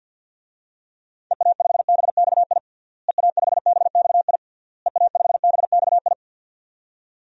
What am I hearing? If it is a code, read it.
Morse code, heard as a keyed tone, about 47 wpm, 700 Hz. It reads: EA5BXI EA5BXI EA5BXI
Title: DXCC-100 Call Signs - 6 Characters - Repeated 3x - 50wpm